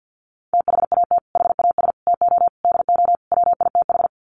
Morse code, heard as a keyed tone, about 50 wpm, 700 Hz. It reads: THAT HAS TO DO WITH
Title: Sets of 5 Words Encore2 50wpm